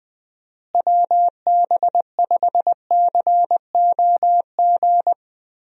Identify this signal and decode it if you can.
Morse code, heard as a keyed tone, about 19 wpm, 700 Hz. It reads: WB5COG
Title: USA Call Signs - 6 Characters - Repeated 3x - 20wpm